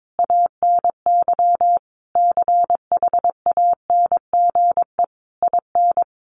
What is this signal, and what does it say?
Morse code, heard as a keyed tone, about 22 wpm, 700 Hz. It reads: ANY CHANGE IN
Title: Sentences from Top 1200 Words 22wpm